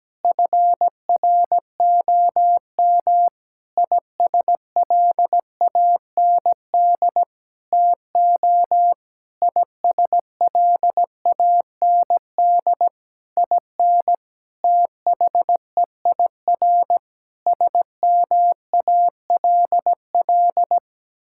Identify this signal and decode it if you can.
Morse code, heard as a keyed tone, about 17 wpm, 700 Hz. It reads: FROM ISLAND TO ISLAND IN THEIR SMALL
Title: Sentences from Top 1000 Words - Review Part 6 - 17wpm